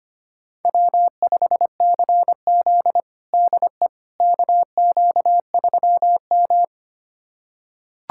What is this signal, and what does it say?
Morse code, heard as a keyed tone, about 25 wpm, 700 Hz. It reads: W5CZ DE KQ3M
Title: USA Callsign DE Callsign - 4 Characters - 25wpm